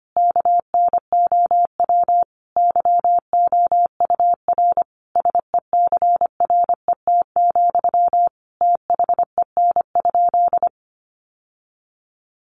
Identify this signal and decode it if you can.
Morse code, heard as a keyed tone, about 25 wpm, 700 Hz. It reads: KNOW YOUR SECRET, THEN?
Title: Sentences from Top 1200 Words 25wpm